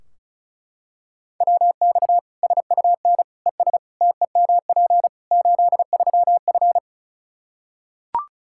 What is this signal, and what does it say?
Morse code, heard as a keyed tone, about 36 wpm, 700 Hz. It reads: WX SUN ES TEMP 83F
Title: K1USN SST QSO Practice 35wpm